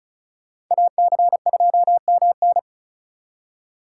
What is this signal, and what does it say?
Morse code, heard as a keyed tone, about 45 wpm, 700 Hz. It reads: AC2MN